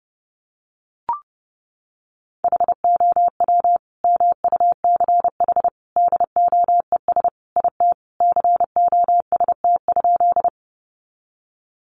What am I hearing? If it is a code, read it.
Morse code, heard as a keyed tone, about 30 wpm, 700 Hz. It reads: HOW MUCH DOES IT COST?
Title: Sentences from Top 800 Words 30wpm